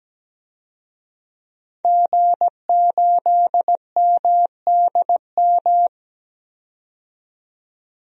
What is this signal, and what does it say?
Morse code, heard as a keyed tone, about 17 wpm, 700 Hz. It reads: G8MDM
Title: DXCC-100 Call Signs - 5 Characters - 17wpm